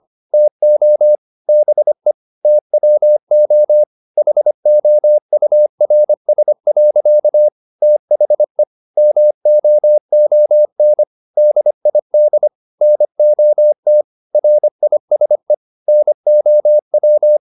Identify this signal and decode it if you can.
Morse code, heard as a keyed tone, about 25 wpm, 600 Hz. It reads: TO BE TWO HOURS. THE MOON DID NOT RISE NOW